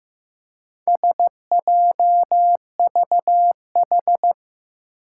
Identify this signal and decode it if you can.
Morse code, heard as a keyed tone, about 15 wpm, 700 Hz. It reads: SJVH
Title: Four Challenging Letter-Number Combinations 15wpm